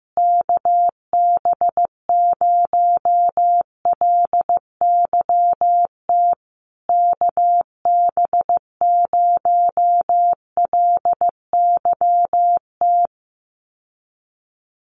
Morse code, heard as a keyed tone, about 15 wpm, 700 Hz. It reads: KB0LYT KB0LYT
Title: USA Call Signs Repeated 3x 15wpm